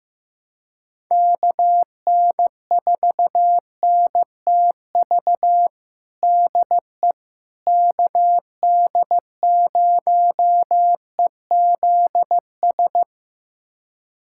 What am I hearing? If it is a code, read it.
Morse code, heard as a keyed tone, about 15 wpm, 700 Hz. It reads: KN4NTV DE KD0EZS
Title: USA Callsign DE Callsign 15wpm